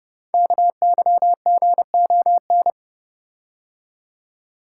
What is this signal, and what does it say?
Morse code, heard as a keyed tone, about 30 wpm, 700 Hz. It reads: KYGON
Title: Code Groups - 5 Letters - 30wpm